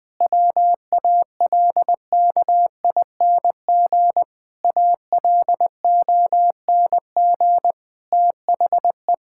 Morse code, heard as a keyed tone, about 20 wpm, 700 Hz. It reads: WALKING ALONG THE
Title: Difficult Sentences with Hints - 20wpm